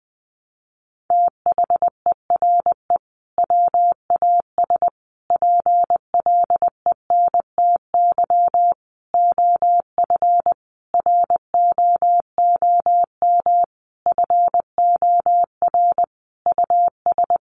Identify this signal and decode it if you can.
Morse code, heard as a keyed tone, about 20 wpm, 700 Hz. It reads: THERE WAS PLENTY OF ROOM FOR US
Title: Sentences from Top 1300 Words 20wpm